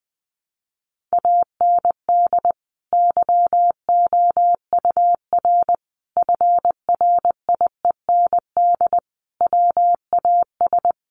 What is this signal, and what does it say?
Morse code, heard as a keyed tone, about 20 wpm, 700 Hz. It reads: AND YOUR FRIEND WAS
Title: Sentences from Top 1400 Words 20wpm